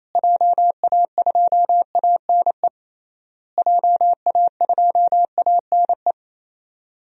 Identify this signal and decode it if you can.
Morse code, heard as a keyed tone, about 28 wpm, 700 Hz. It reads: JA2ANE JA2ANE
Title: DXCC-100 Call Signs - 6 Characters - Repeated 3x - 28wpm